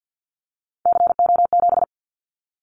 Morse code, heard as a keyed tone, about 48 wpm, 700 Hz. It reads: CO7
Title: Callsign Prefixes for Top 100 DXCC - 50wpm